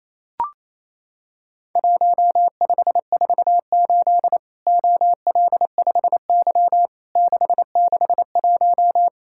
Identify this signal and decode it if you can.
Morse code, heard as a keyed tone, about 28 wpm, 700 Hz. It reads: 1548 OL5Y 661
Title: Worked All Europe DX Contest - QTC Exchange - 28wpm